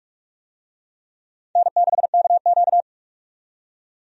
Morse code, heard as a keyed tone, about 45 wpm, 700 Hz. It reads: N6KQ
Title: USA Call Signs - 4 Characters - 45wpm